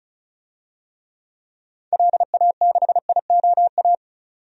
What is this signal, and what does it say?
Morse code, heard as a keyed tone, about 36 wpm, 700 Hz. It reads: LA6IOA
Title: DXCC-100 Call Signs - 6 Characters - 35wpm